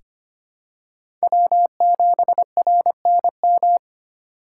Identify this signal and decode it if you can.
Morse code, heard as a keyed tone, about 25 wpm, 700 Hz. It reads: W 7 R N M
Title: USA Call Signs - 8x Character Spacing - 25wpm